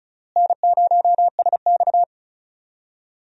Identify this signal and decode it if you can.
Morse code, heard as a keyed tone, about 36 wpm, 700 Hz. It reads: N0SX
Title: USA Call Signs - 4 Characters - Repeated 3x - 35wpm